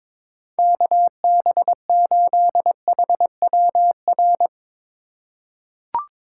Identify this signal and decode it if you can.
Morse code, heard as a keyed tone, about 22 wpm, 700 Hz. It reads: KB8HWR
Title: USA Call Signs - 6 Characters - 3x Character Spacing - 22wpm